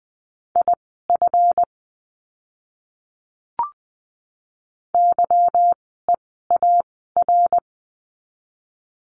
Morse code, heard as a keyed tone, about 20 wpm, spaced as for 15 wpm, 700 Hz. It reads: IF YEAR
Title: 185 Lesson 30f - Letter b - Words 20wpm